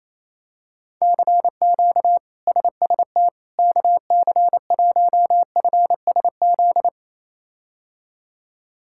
Morse code, heard as a keyed tone, about 28 wpm, 700 Hz. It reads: CQ SST KC1FSZ
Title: K1USN SST QSO Practice - 28wpm